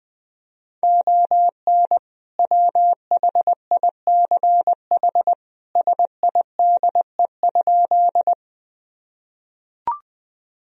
Morse code, heard as a keyed tone, about 21 wpm, 700 Hz. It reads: ON WHICH SIDE?
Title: Sentences from Top 1000 Words - Review Part 1 - 20wpm